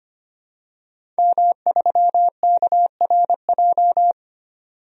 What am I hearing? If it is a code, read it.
Morse code, heard as a keyed tone, about 25 wpm, 700 Hz. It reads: M3KRJ
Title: DXCC-100 Call Signs - 5 Characters - 25wpm